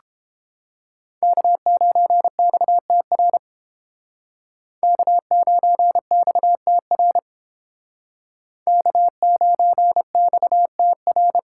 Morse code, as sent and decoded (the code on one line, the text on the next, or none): -.- ----. -..- - .-. / -.- ----. -..- - .-. / -.- ----. -..- - .-.
K9XTR K9XTR K9XTR